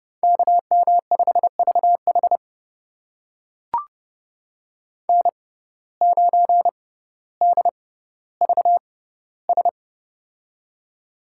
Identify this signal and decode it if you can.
Morse code, heard as a keyed tone, about 30 wpm, spaced as for 15 wpm, 700 Hz. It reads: KM5VH N9DVS
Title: USA Call Signs - 5 Characters - 6x Character Spacing - 30wpm